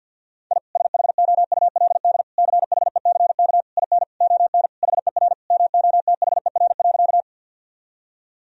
Morse code, heard as a keyed tone, about 52 wpm, 700 Hz. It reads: I SHOULD CHECK IN ON HER MOTHER.
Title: Sentences from Top 1000 Words - Review Part 3 - 50wpm